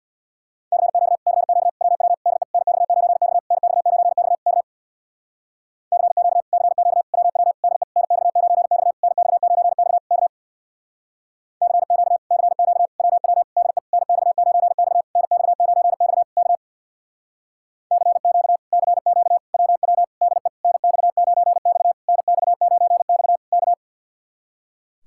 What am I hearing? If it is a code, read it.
Morse code, heard as a keyed tone, about 64 wpm, 700 Hz. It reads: CQ CQ WW DE NX9X NX9X K CQ CQ WW DE NX9X NX9X K CQ CQ WW DE NX9X NX9X K CQ CQ WW DE NX9X NX9X K